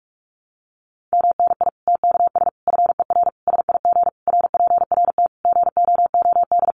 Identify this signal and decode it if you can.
Morse code, heard as a keyed tone, about 44 wpm, 700 Hz. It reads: MNI TKS FER SIG RPRT GOOD
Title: Common QSO Phrases 45wpm